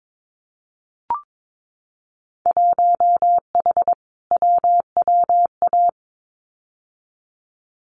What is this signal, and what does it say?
Morse code, heard as a keyed tone, about 22 wpm, 700 Hz. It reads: 1H WWA
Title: Winter Field Day Exchanges 22wpm